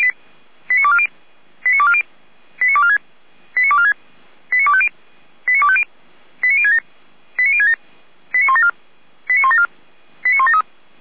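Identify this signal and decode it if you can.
Five-tone selective call (ZVEI), heard as quick runs of five tones, tones of about 70 ms. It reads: 89240 89240 89246 89246 89250 89250 89067 89067 89174 89174 89173
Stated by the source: ZVEI five-tone selective call received on SDR, archived by the Feuerwehr Orlamünde fire brigade.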